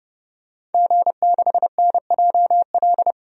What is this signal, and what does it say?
Morse code, heard as a keyed tone, about 30 wpm, 700 Hz. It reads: G6NJL
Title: DXCC-100 Call Signs - 5 Characters - 30wpm